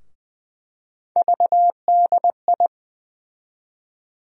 Morse code, heard as a keyed tone, about 20 wpm, 700 Hz. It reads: VDI
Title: USA Call Sign Postfixes - 3 Letters 20wpm